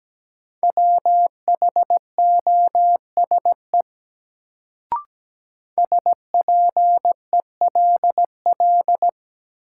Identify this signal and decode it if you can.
Morse code, heard as a keyed tone, about 17 wpm, 700 Hz. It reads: WHOSE SPELL
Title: Sentences from Top 1200 Words - New Word Drill Alt - 17wpm